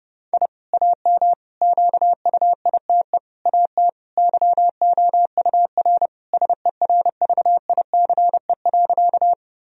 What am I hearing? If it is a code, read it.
Morse code, heard as a keyed tone, about 30 wpm, 700 Hz. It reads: I AM QUITE AT YOUR SERVICE.
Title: Sentences from Top 1000 Words - Review Part 4 - 30wpm